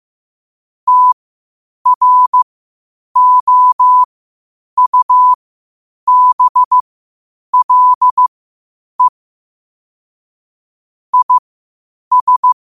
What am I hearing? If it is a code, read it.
Morse code, heard as a keyed tone, about 15 wpm, spaced as for 8 wpm, 1000 Hz. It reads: TROUBLE IS
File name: qotdfiles / 8wpm